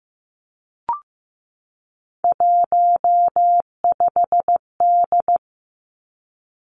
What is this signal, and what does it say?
Morse code, heard as a keyed tone, about 15 wpm, 700 Hz. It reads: 15D
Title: Three Challenging Letter-Number Combinations 15wpm